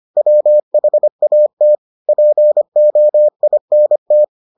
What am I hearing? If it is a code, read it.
Morse code, heard as a keyed tone, about 25 wpm, 600 Hz. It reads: WHAT POINT